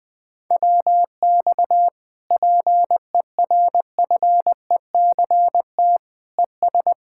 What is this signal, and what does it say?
Morse code, heard as a keyed tone, about 20 wpm, 700 Hz. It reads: WX PERFECT ES
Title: Speed-Racing Common QSO Phrases Encore - 20wpm